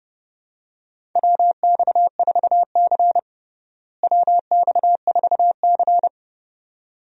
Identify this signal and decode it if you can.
Morse code, heard as a keyed tone, about 30 wpm, 700 Hz. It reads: WX4C WX4C
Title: USA Call Signs - 4 Characters - Repeated 2x - 30wpm